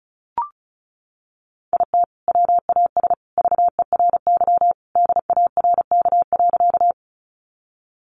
Morse code, heard as a keyed tone, about 34 wpm, 700 Hz. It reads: IT WAS VERY DARK.
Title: Sentences from Top 500 Words 35wpm